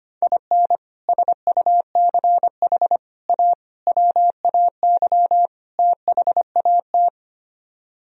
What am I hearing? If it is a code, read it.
Morse code, heard as a keyed tone, about 30 wpm, 700 Hz. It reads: IN SUCH A WAY THAT
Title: Speed-Racing - Sets of 5 Words 25wpm